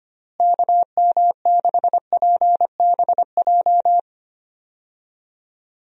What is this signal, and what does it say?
Morse code, heard as a keyed tone, about 25 wpm, 700 Hz. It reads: KM6PBJ
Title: USA Call Signs - 6 Characters - Repeated 2x - 25wpm